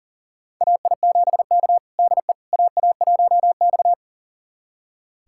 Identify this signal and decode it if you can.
Morse code, heard as a keyed tone, about 40 wpm, 700 Hz. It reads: AI7K DE AA1X
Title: USA Callsign DE Callsign - 4 Characters - 40wpm